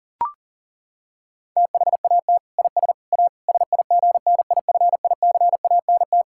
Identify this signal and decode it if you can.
Morse code, heard as a keyed tone, about 40 wpm, 700 Hz. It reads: THAT IS A SIGNIFICANT
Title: Sentences from Top 1000 Words - Review Part 6 - 40wpm